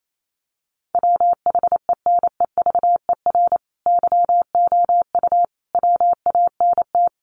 Morse code, heard as a keyed tone, about 28 wpm, 700 Hz. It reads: WHENEVER YOU WANT
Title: Sentences from Top 1200 Words 28wpm